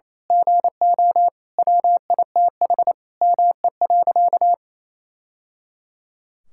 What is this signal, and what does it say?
Morse code, heard as a keyed tone, about 28 wpm, 700 Hz. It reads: GO WITH ME.
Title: Sentences from Top 1000 Words - Review Part 6 - 28wpm